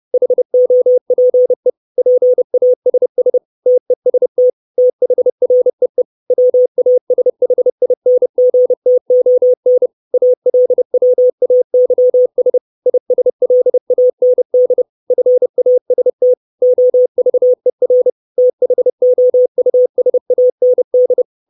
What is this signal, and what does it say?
Morse code, heard as a keyed tone, about 30 wpm, 500 Hz. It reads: HOPE PASS TEST THREE WASHINGTON ALWAYS ISLAND FAST OVER THOUSAND
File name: 632 Words 30WPM